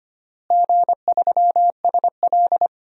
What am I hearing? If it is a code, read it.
Morse code, heard as a keyed tone, about 25 wpm, 700 Hz. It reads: G3SL
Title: DXCC-100 Call Signs - 4 Characters - Repeated 3x - 25wpm